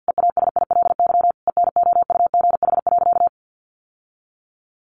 Morse code, heard as a keyed tone, about 50 wpm, 700 Hz. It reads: EASILY ENOUGH.
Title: Sentences from Top 900 Words 50wpm